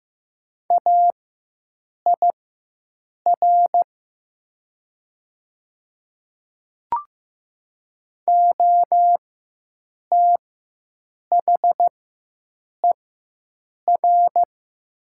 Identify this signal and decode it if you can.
Morse code, heard as a keyed tone, about 15 wpm, spaced as for 7 wpm, 700 Hz. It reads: AIR OTHER
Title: 4x Character Spacing - Top 300 Words - 15wpm